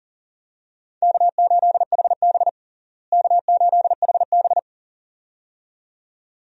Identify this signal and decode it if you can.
Morse code, heard as a keyed tone, about 40 wpm, 700 Hz. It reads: K8HB K8HB
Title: USA Call Signs - 4 Characters - Repeated 2x - 40wpm